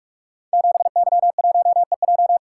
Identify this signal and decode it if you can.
Morse code, heard as a keyed tone, about 48 wpm, 700 Hz. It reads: 7Y1EJ
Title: Code Groups - 5 Letter:Numbers - 45wpm